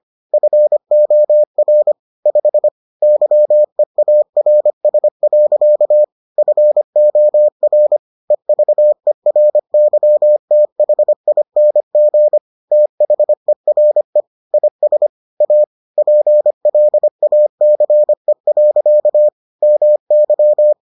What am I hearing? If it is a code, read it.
Morse code, heard as a keyed tone, about 25 wpm, 600 Hz. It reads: FOR 5 YEARS. FOR EVERYTHING THERE IS A PLACE. MY